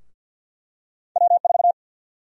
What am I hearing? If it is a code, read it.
Morse code, heard as a keyed tone, about 51 wpm, 700 Hz. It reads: W4
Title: USA Call Sign Prefixes - Repeated 2x - 50wpm